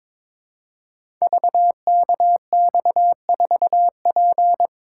Morse code, heard as a keyed tone, about 22 wpm, 700 Hz. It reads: VKX4P
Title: Code Groups - 5 Letter:Numbers - 22wpm